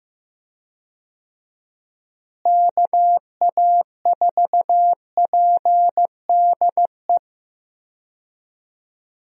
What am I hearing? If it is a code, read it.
Morse code, heard as a keyed tone, about 15 wpm, 700 Hz. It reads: KA4PDE
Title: USA Call Signs - 6 Characters - Repeated 2x - 15wpm